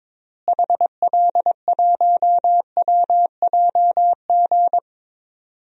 Morse code, heard as a keyed tone, about 22 wpm, 700 Hz. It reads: HL1WJG
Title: DXCC-100 Call Signs - 6 Characters - Repeated 2x - 22wpm